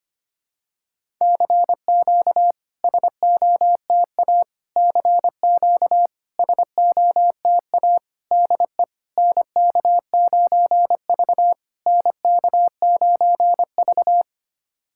Morse code, heard as a keyed tone, about 25 wpm, 700 Hz. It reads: CQ SOTA CQ SOTA DE NK9V NK9V
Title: SOTA QSO Practice 25wpm